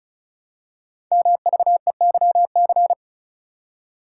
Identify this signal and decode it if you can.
Morse code, heard as a keyed tone, about 36 wpm, 690 Hz. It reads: MVEYC
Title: Code Groups - 5 Letters - 35wpm